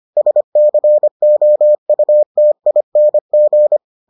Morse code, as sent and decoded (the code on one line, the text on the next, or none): ... -.-. --- ..- - .. -. --.
SCOUTING